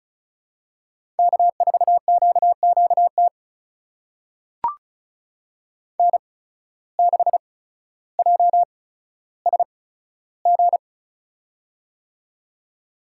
Morse code, heard as a keyed tone, about 36 wpm, spaced as for 15 wpm, 700 Hz. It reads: K4QQT N6JSG
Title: USA Call Signs - 5 Characters - 8x Character Spacing - 35wpm